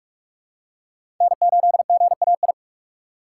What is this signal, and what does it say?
Morse code, heard as a keyed tone, about 46 wpm, 700 Hz. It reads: N8GAI
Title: USA Call Signs - 5 Characters - 4x Character Spacing - 45wpm